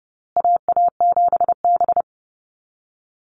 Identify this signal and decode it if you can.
Morse code, heard as a keyed tone, about 30 wpm, 700 Hz. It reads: AA7B
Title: USA Call Signs Repeated 2x 30wpm